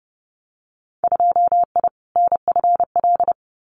Morse code, heard as a keyed tone, about 30 wpm, 700 Hz. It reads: 2I NFL
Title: Winter Field Day Exchanges 30wpm